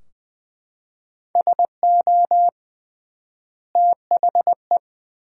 Morse code, heard as a keyed tone, about 21 wpm, 700 Hz. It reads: SO THE
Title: 3x Word-Spacing - Sentences from Top 400 Words - 20wpm